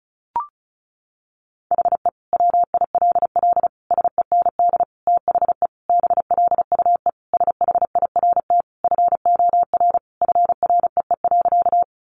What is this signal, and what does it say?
Morse code, heard as a keyed tone, about 34 wpm, 700 Hz. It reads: HE WILL SEND THE BLUE SHIRT FOR FREE.
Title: Additional Easy Phrases 35wpm